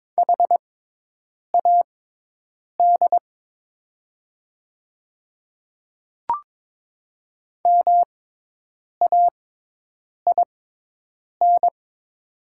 Morse code, heard as a keyed tone, about 22 wpm, spaced as for 8 wpm, 700 Hz. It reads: HAD MAIN
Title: 6x Character Spacing - Top 300 Words - 22wpm